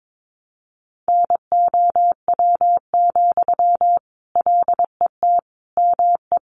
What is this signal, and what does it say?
Morse code, heard as a keyed tone, about 22 wpm, 700 Hz. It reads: NOW, LET ME
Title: Sentences from Top 800 Words 22wpm